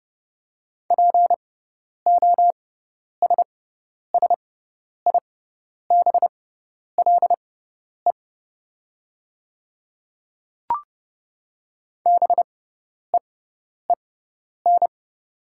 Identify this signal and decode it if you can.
Morse code, heard as a keyed tone, about 30 wpm, spaced as for 10 wpm, 700 Hz. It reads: POSSIBLE BEEN
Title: 6x Character Spacing - Top 500 Words - 30wpm